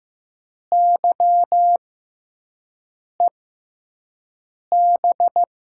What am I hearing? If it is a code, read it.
Morse code, heard as a keyed tone, about 15 wpm, spaced as for 5 wpm, 700 Hz. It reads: YEB
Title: USA Call Signs - 6 Characters - 6x Character Spacing - 15wpm